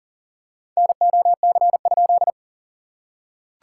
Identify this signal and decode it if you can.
Morse code, heard as a keyed tone, about 40 wpm, 700 Hz. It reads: NOC?
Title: SOTA QSO Practice 40wpm